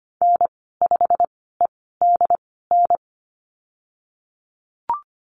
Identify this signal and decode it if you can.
Morse code, heard as a keyed tone, about 25 wpm, spaced as for 16 wpm, 700 Hz. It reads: N5EDN
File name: Lesson 15g - Number 5 - Call Signs 25wpm